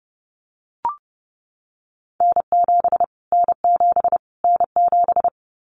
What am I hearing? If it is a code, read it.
Morse code, heard as a keyed tone, about 30 wpm, 700 Hz. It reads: N7 N7 N7
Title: Callsign Prefixes for Top 25 DXCC - Repeated 3x - 30wpm